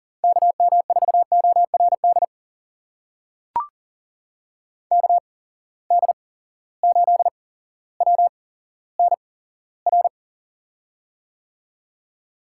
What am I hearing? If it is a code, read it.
Morse code, heard as a keyed tone, about 40 wpm, spaced as for 17 wpm, 700 Hz. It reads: KM4ORD KD8WNR
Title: USA Call Signs - 6 Characters - 8x Character Spacing - 40wpm